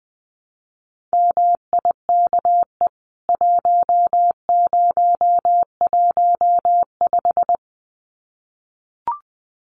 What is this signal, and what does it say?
Morse code, heard as a keyed tone, about 20 wpm, 700 Hz. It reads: MIKE 1015
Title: CWT Exchange 20wpm